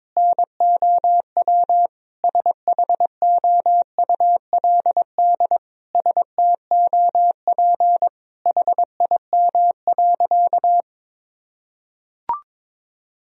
Morse code, heard as a keyed tone, about 21 wpm, 700 Hz. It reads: NOW SHOULD STOP HIM.
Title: Sentences from Top 1000 Words - Review Part 1 - 22wpm